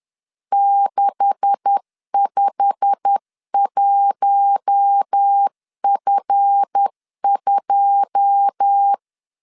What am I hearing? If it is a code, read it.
Morse code, heard as a keyed tone, about 10 wpm, 800 Hz. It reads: 651F2
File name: LEZIONI / 9 286 10wpm 2735